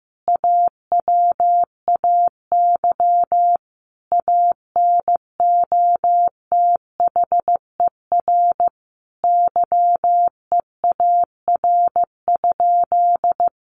Morse code, heard as a keyed tone, about 15 wpm, 700 Hz. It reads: AWAY ANOTHER YEAR?
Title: Sentences from Top 1500 Words 15wpm